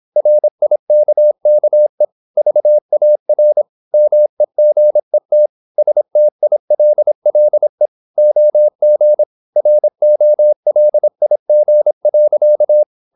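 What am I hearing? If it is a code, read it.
Morse code, heard as a keyed tone, about 26 wpm, 600 Hz. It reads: RIKKE VAR MEGET STILLE OG ROLIG.
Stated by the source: High-speed (QRQ) Morse at 26 wpm.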